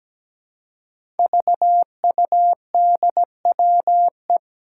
Morse code, heard as a keyed tone, about 17 wpm, 700 Hz. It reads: VUDWE
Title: Five Challenging Letter-Number Combinations 17wpm